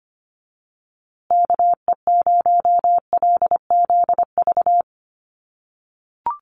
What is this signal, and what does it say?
Morse code, heard as a keyed tone, about 25 wpm, 700 Hz. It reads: KE0LZV
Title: USA Call Signs Repeated 2x 25wpm